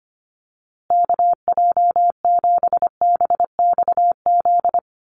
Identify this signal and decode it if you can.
Morse code, heard as a keyed tone, about 25 wpm, 700 Hz. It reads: KJ7BXZ
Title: USA Call Signs Repeated 3x 25wpm